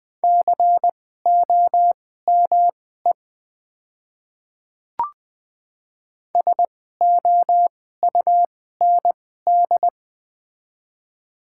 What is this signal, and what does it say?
Morse code, heard as a keyed tone, about 20 wpm, spaced as for 15 wpm, 700 Hz. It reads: COME SOUND
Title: 2x Character Spacing - Top 100 Words - 20wpm